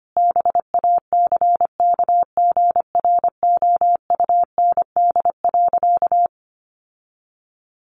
Morse code, heard as a keyed tone, about 25 wpm, 700 Hz. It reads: BACKGROUND.
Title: Sentences from Top 1300 Words 25wpm